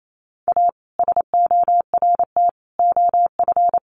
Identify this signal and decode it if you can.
Morse code, heard as a keyed tone, about 28 wpm, 700 Hz. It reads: A SORT OF
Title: Sentences from Top 1300 Words 28wpm